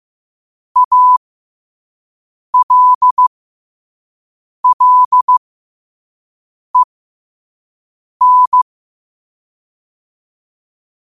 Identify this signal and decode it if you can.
Morse code, heard as a keyed tone, about 15 wpm, spaced as for 5 wpm, 1000 Hz. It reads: ALLEN
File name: qotdfiles / 5wpm